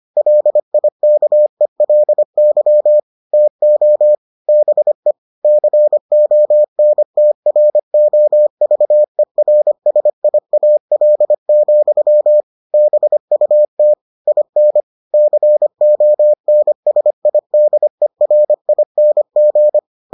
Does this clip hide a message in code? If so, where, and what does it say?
Yes: Morse code from 0.2 to 19.8 s, heard as a keyed tone, about 25 wpm, 600 Hz. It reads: LIKELY TO BE CONTROVERSIAL, BUT IN CONSIDERING